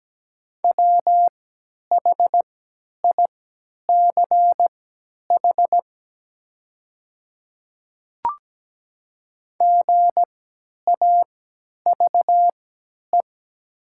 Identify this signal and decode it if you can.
Morse code, heard as a keyed tone, about 17 wpm, spaced as for 10 wpm, 700 Hz. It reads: WHICH GAVE
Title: 3x Character Spacing - Top 500 Words - 17wpm